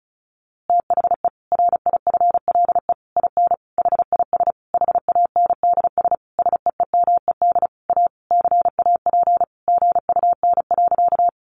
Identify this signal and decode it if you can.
Morse code, heard as a keyed tone, about 34 wpm, 700 Hz. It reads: THE RIFLE IN HIS HANDS SEEMED A CAP GUN.
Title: A Sound Of Thunder 35wpm